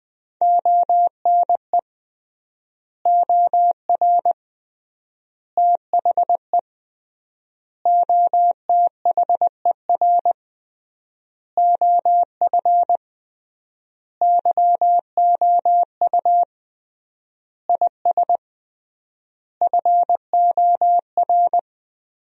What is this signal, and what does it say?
Morse code, heard as a keyed tone, about 20 wpm, 700 Hz. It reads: ONE OR THE OTHER OF YOU IS FOR
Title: 3x Word-Spacing - Sentences from Top 400 Words - 20wpm